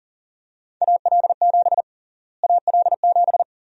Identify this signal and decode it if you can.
Morse code, heard as a keyed tone, about 40 wpm, 700 Hz. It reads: AL7 AL7
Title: USA Call Sign Prefixes - Repeated 2x - 40wpm